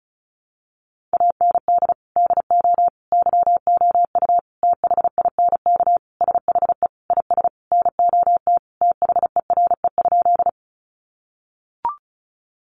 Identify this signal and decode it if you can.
Morse code, heard as a keyed tone, about 34 wpm, 700 Hz. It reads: AND DO YOU THINK SHE IS NOT THERE?
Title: Sentences from Top 600 Words 35wpm